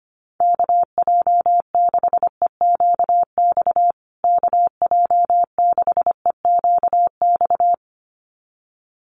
Morse code, heard as a keyed tone, about 25 wpm, 700 Hz. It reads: KJ6EQX KJ6EQX
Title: USA Call Signs Repeated 2x 25wpm